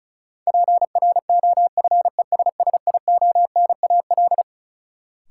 Morse code, heard as a keyed tone, about 36 wpm, 700 Hz. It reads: PROFESSIONAL
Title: Sentences from Top 1400 Words - New Word Drill Alt - 35wpm